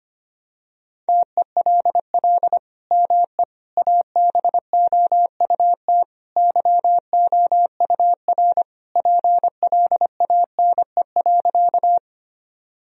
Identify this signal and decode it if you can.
Morse code, heard as a keyed tone, about 25 wpm, 700 Hz. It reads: TELL ME ABOUT YOUR PLANE.
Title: Sentences from Top 1000 Words - Review Part 3 - 25wpm